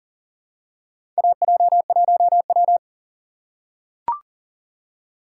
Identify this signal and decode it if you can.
Morse code, heard as a keyed tone, about 40 wpm, 700 Hz. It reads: AJ1W
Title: USA Call Signs - 4 Characters - 2x Character Spacing - 40wpm